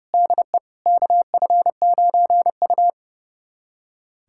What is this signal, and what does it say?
Morse code, heard as a keyed tone, about 32 wpm, 700 Hz. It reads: DE KF9U
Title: USA Callsign DE Callsign - 4 Characters - 30wpm